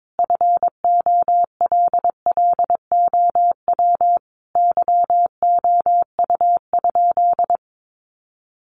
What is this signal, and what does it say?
Morse code, heard as a keyed tone, about 22 wpm, 700 Hz. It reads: FOLLOW YOU?
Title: Sentences from Top 200 Words 22wpm